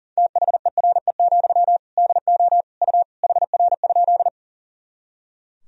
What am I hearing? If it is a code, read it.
Morse code, heard as a keyed tone, about 40 wpm, 700 Hz. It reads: THERE, DO U HR?
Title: Sentences from Top 300 Words and Abbreviations 40wpm